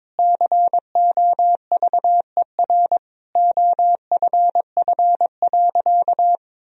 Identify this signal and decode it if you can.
Morse code, heard as a keyed tone, about 22 wpm, 700 Hz. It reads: COVER OFF.
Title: Sentences from Top 1000 Words - Review Part 1 - 22wpm